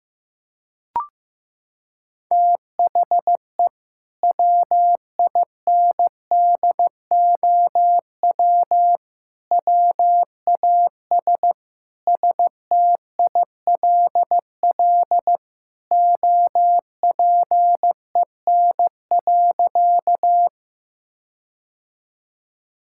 Morse code, heard as a keyed tone, about 15 wpm, 700 Hz. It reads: THE WINDOW WAS STILL OPEN.
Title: Sentences from Top 1600 Words 15wpm